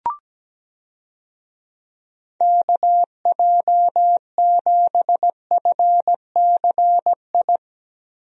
Morse code, heard as a keyed tone, about 17 wpm, 700 Hz. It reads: KJ7FCI
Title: USA Call Signs - 6 Characters - Repeated 3x - 17wpm